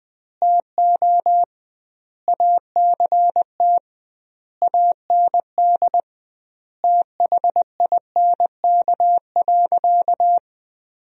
Morse code, heard as a keyed tone, about 20 wpm, 700 Hz. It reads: TO ACT AND THINK.
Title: 2x Word-Spacing - Sentences from Top 500 Words - 20wpm